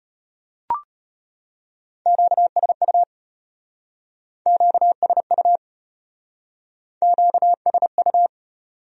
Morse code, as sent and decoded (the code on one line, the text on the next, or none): --.- ... ..- / --.- ... ..- / --.- ... ..-
QSU QSU QSU